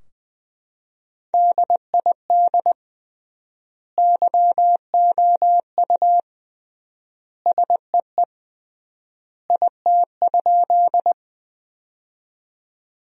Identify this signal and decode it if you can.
Morse code, heard as a keyed tone, about 20 wpm, 700 Hz. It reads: DID YOU SEE IT?
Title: 3x Word-Spacing - Sentences from Top 100 Words - 20wpm